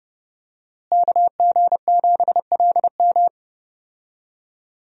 Morse code, heard as a keyed tone, about 30 wpm, 700 Hz. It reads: KG7LM
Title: USA Call Signs - 5 Characters - Repeated 2x - 30wpm